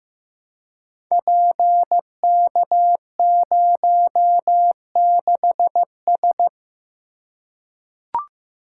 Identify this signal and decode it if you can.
Morse code, heard as a keyed tone, about 15 wpm, 700 Hz. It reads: PK06S
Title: Code Groups - 5 Letter:Numbers - 15wpm